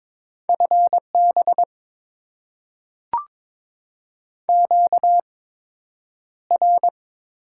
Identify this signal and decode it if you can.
Morse code, heard as a keyed tone, about 22 wpm, spaced as for 9 wpm, 700 Hz. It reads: FB QR
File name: Top 90 QSO Elements - 8x Character Spacing - 22wpm